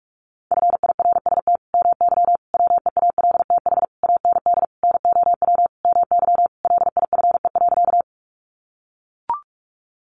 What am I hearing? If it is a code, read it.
Morse code, heard as a keyed tone, about 44 wpm, 700 Hz. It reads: FIRST MY WEALTH AND NOW MY LIFE.
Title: Sentences from Top 1200 Words 45wpm